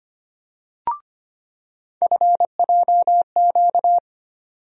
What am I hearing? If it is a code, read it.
Morse code, heard as a keyed tone, about 25 wpm, 700 Hz. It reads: FJQ
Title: USA Call Sign Postfixes - 3 Letters 25wpm